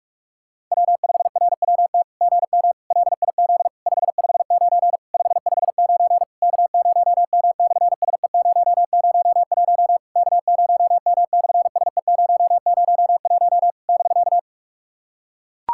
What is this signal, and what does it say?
Morse code, heard as a keyed tone, about 46 wpm, 700 Hz. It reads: W5RWT GM LIZ 559 559 K0M/SE001 K0M/SE001 <BK>
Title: SOTA QSO Practice 45wpm